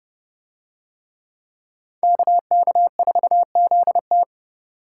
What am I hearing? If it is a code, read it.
Morse code, heard as a keyed tone, about 30 wpm, 700 Hz. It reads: KK4ZT